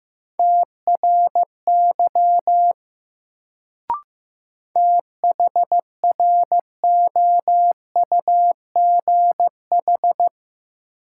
Morse code, heard as a keyed tone, about 15 wpm, 700 Hz. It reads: TRY THROUGH
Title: Sets of 2 Words - New Word Drill - 15wpm